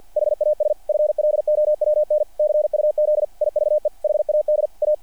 Morse code, heard as a keyed tone, about 48 wpm, 600 Hz. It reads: HAS FLOWN CUZ IVE HAD A
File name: archived / CRSnet